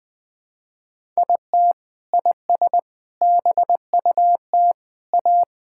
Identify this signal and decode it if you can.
Morse code, heard as a keyed tone, about 20 wpm, 700 Hz. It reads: IT IS BUT A
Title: Sentences from Top 1000 Words - Review Part 6 - 20wpm